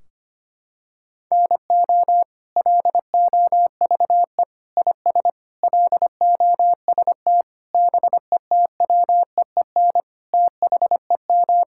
Morse code, heard as a keyed tone, about 25 wpm, 700 Hz. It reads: NO LOVE IS LOST BETWEEN THEM
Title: Sentences from Top 1600 Words 25wpm